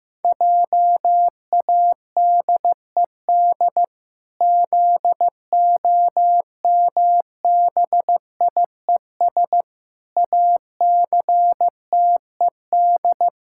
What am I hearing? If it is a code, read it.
Morse code, heard as a keyed tone, about 15 wpm, 700 Hz. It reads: JADED ZOMBIES ACTED
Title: Pangram Challenge v2 - 15wpm